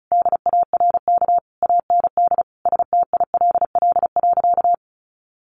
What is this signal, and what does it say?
Morse code, heard as a keyed tone, about 34 wpm, 700 Hz. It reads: DARK AND STILL.
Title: Sentences from Top 500 Words 35wpm